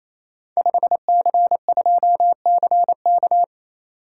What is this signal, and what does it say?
Morse code, heard as a keyed tone, about 28 wpm, 700 Hz. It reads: 5C2CK
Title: Code Groups - 5 Letter:Numbers - 28wpm